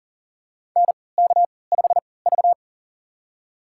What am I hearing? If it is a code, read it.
Morse code, heard as a keyed tone, about 40 wpm, spaced as for 23 wpm, 700 Hz. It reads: NK5V
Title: USA Call Signs - 3x Character Spacing - 40wpm